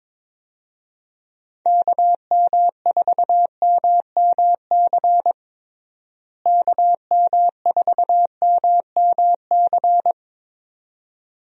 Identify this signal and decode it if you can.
Morse code, heard as a keyed tone, about 22 wpm, 700 Hz. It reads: KM4MMC KM4MMC
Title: USA Call Signs - 6 Characters - Repeated 2x - 22wpm